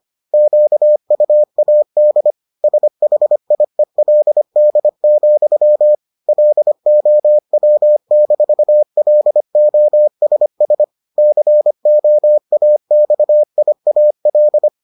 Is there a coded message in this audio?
Yes: Morse code from 0.3 to 14.7 s, heard as a keyed tone, about 25 wpm, 600 Hz. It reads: QUAD SHIELD, LOW-LOSS COAXIAL